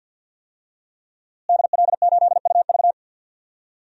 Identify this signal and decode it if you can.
Morse code, heard as a keyed tone, about 52 wpm, 700 Hz. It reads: DL8UV
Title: DXCC-100 Call Signs - 5 Characters - Repeated 2x - 50wpm